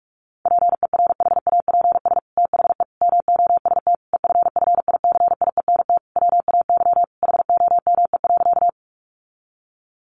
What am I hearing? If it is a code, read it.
Morse code, heard as a keyed tone, about 46 wpm, 700 Hz. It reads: PERHAPS THE MOST EFFICIENT WAY HOME.
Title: Sentences from Top 1300 Words 45wpm